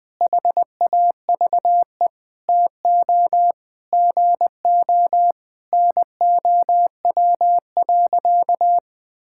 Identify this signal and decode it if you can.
Morse code, heard as a keyed tone, about 21 wpm, 700 Hz. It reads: HAVE TO GO NOW.
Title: Sentences with 5-Word Phrases - Spoken Beginning - 20wpm